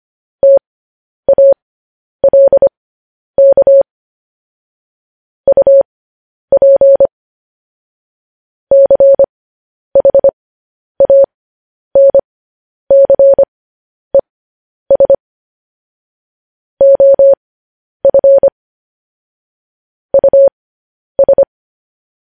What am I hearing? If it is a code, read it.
Morse code, heard as a keyed tone, about 25 wpm, spaced as for 10 wpm, 550 Hz. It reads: TALK UP CHANCES OF US
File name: podcast / News headlines-1x-25x10